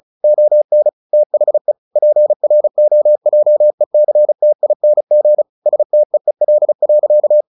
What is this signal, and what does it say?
Morse code, heard as a keyed tone, about 34 wpm, 600 Hz. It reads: ON THE PROJECTING STEEL.
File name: the old man and the sea / the old man and the sea 1 35wpm